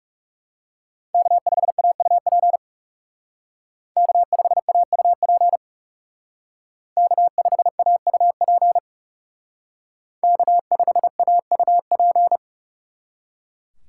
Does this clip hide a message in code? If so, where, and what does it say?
Yes: Morse code from 1.1 to 12.4 s, heard as a keyed tone, about 38 wpm, 700 Hz. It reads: K5AUP K5AUP K5AUP K5AUP